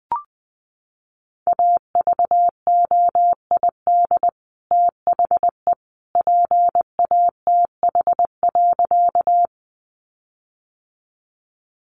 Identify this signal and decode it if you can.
Morse code, heard as a keyed tone, about 20 wpm, 700 Hz. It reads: AVOID THE PATH.
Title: Sentences from Top 1400 Words 20wpm